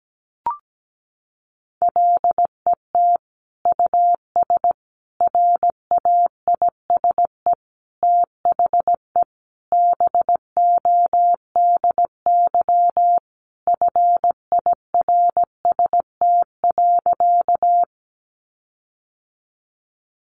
Morse code, heard as a keyed tone, about 17 wpm, 700 Hz. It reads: LET US RAISE THE BODY FIRST.
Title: Sentences from Top 1400 Words 17wpm